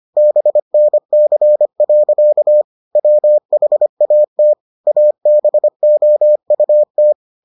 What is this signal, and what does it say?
Morse code, heard as a keyed tone, about 25 wpm, 600 Hz. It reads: BNC. WHAT ABOUT